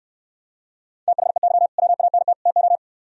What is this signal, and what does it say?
Morse code, heard as a keyed tone, about 69 wpm, 700 Hz. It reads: THEY WANT TO